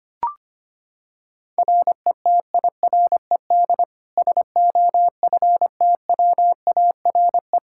Morse code, heard as a keyed tone, about 25 wpm, 700 Hz. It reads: RETIRED SOFTWARE
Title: Practice QSO - Exchange 3 - More Info - 25wpm